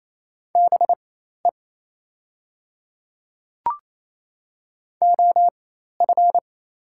Morse code, heard as a keyed tone, about 28 wpm, spaced as for 13 wpm, 700 Hz. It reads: BE OF
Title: 4x Character Spacing - Top 100 Words - 28wpm